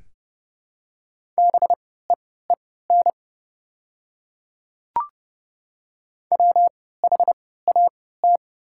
Morse code, heard as a keyed tone, about 30 wpm, spaced as for 17 wpm, 690 Hz. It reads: BEEN WHAT
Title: 3x Character Spacing - Top 200 Words - 30wpm